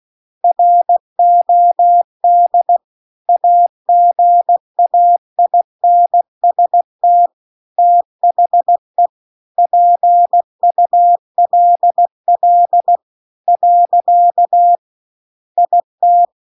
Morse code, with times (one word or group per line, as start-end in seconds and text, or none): 0.4-2.8 ROD
3.3-7.3 AGAINST
7.8-9.1 THE
9.6-13.0 PULL
13.5-14.8 .
15.6-16.2 IT